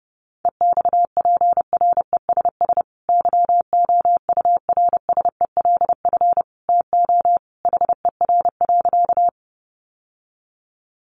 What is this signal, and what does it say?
Morse code, heard as a keyed tone, about 28 wpm, 700 Hz. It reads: EXPRESS YOURSELF TO HER.
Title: Sentences from Top 1100 Words 30wpm